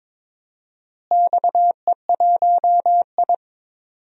Morse code, heard as a keyed tone, about 22 wpm, 700 Hz. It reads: XE1I
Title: DXCC-100 Call Signs - 4 Characters - Repeated 3x - 22wpm